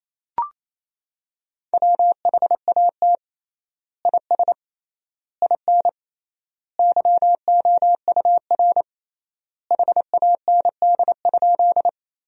Morse code, heard as a keyed tone, about 28 wpm, 700 Hz. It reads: WHAT IS IN YOUR HAND?
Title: 3x Word-Spacing - Sentences from Top 200 Words - 28wpm